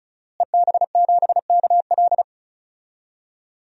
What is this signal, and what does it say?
Morse code, heard as a keyed tone, about 34 wpm, 700 Hz. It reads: EB7KL
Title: DXCC-100 Call Signs - 5 Characters - Repeated 2x - 35wpm